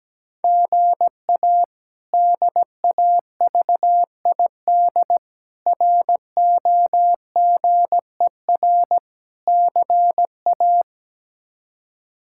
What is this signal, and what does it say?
Morse code, heard as a keyed tone, about 17 wpm, 700 Hz. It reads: GA DAVID ROGER CA
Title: K1USN SST QSO Practice - 17wpm